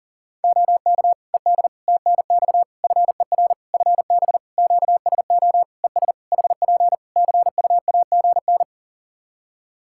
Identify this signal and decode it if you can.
Morse code, heard as a keyed tone, about 40 wpm, 700 Hz. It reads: OK ED TNX FER FB QSO ES HP CUAGN
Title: Practice QSO - Exchange 4 - Ending - 40wpm